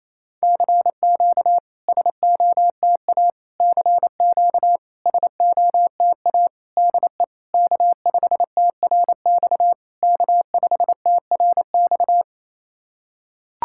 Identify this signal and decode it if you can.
Morse code, heard as a keyed tone, about 28 wpm, 700 Hz. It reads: CQ SOTA CQ SOTA DE K5TRX K5TRX
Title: SOTA QSO Practice 28wpm